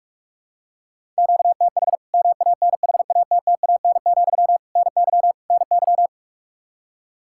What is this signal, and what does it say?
Morse code, heard as a keyed tone, about 41 wpm, 700 Hz. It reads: QTH MANHATTAN, NY NY
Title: Practice QSO - Exchange 1 - Beginning - 45wpm